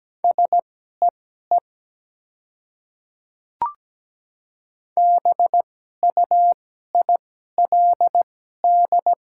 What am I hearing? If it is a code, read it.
Morse code, heard as a keyed tone, about 17 wpm, spaced as for 12 wpm, 700 Hz. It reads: SEE BUILD
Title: 2x Character Spacing - Top 200 Words - 17wpm